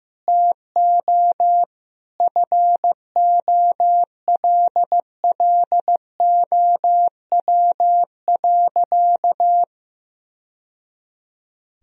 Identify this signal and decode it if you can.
Morse code, heard as a keyed tone, about 16 wpm, 700 Hz. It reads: TO FOLLOW.
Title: Sentences from Top 1000 Words - Review Part 2 - 15wpm